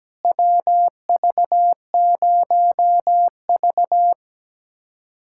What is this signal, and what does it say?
Morse code, heard as a keyed tone, about 17 wpm, 700 Hz. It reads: WV0V
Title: USA Callsign DE Callsign - 4 Characters - 17wpm